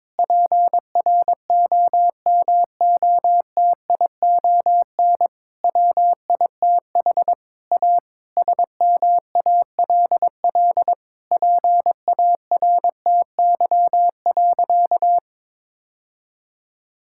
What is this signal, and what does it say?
Morse code, heard as a keyed tone, about 23 wpm, 700 Hz. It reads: PROMOTION WITH A SMALL PARTY.
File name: Difficult Sentences with Hints - 22wpm